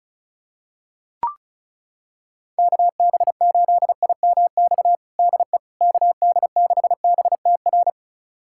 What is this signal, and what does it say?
Morse code, heard as a keyed tone, about 34 wpm, 700 Hz. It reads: KB8IMX DE KD6BTR
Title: USA Callsign DE Callsign 35wpm